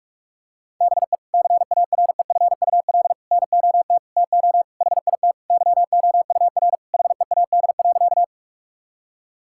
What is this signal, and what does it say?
Morse code, heard as a keyed tone, about 44 wpm, 700 Hz. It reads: BE CAREFUL NOT TO HIT YOUR HEAD.
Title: Sentences from Top 1600 Words 45wpm